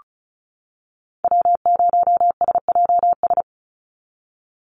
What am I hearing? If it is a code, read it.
Morse code, heard as a keyed tone, about 34 wpm, 700 Hz. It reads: W0SJS
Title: Call Signs - 35wpm